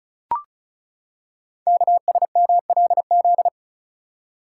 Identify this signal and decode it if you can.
Morse code, heard as a keyed tone, about 36 wpm, 700 Hz. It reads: KSMLZ
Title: Code Groups - 5 Letters - 35wpm